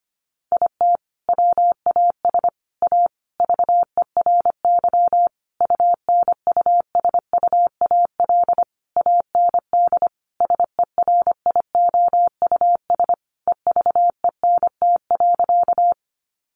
Morse code, heard as a keyed tone, about 25 wpm, 700 Hz. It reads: IT WAS A VERY UNUSUAL AND SERIOUS EVENT.
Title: Sentences from Top 1400 Words 25wpm